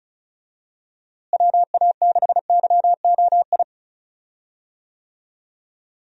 Morse code, heard as a keyed tone, about 36 wpm, 700 Hz. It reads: WA6YOI
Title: USA Call Signs - 6 Characters - Repeated 3x - 35wpm